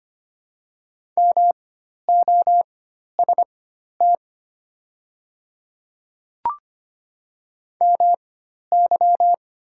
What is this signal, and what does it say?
Morse code, heard as a keyed tone, about 25 wpm, spaced as for 12 wpm, 700 Hz. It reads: MOST MY
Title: 4x Character Spacing - Top 100 Words - 25wpm